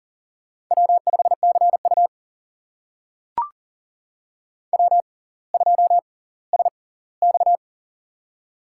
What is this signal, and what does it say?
Morse code, heard as a keyed tone, about 40 wpm, spaced as for 20 wpm, 700 Hz. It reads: W5CU W2SX
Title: USA Call Signs - 4 Characters - 6x Character Spacing - 40wpm